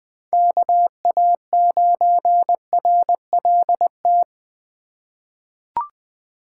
Morse code, heard as a keyed tone, about 20 wpm, 700 Hz. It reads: KA9RLT
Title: USA Call Signs - 6 Characters - 3x Character Spacing - 20wpm